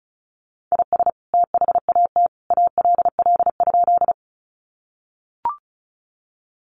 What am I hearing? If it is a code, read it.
Morse code, heard as a keyed tone, about 34 wpm, 700 Hz. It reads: IS THAT ALL?
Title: Sentences from Top 100 Words 35wpm